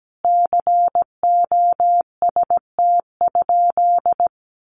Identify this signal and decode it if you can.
Morse code, heard as a keyed tone, about 17 wpm, 700 Hz. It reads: COST?
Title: Sentences from Top 1400 Words 17wpm